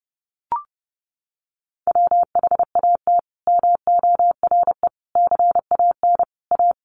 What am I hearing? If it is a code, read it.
Morse code, heard as a keyed tone, about 30 wpm, 700 Hz. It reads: WHAT MORE CAN A
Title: Sentences from Top 1100 Words 30wpm